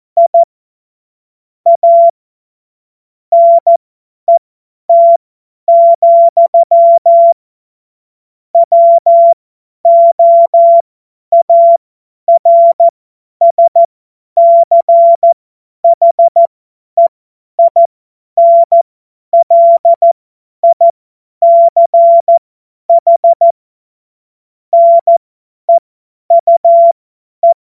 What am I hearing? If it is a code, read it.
Morse code, heard as a keyed tone, about 14 wpm, spaced as for 10 wpm, 680 Hz. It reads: I A NET, WOARSCHEINLICH NEUE
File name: Morsekurs / Witze / Witz 21. Februar 2023 50 F70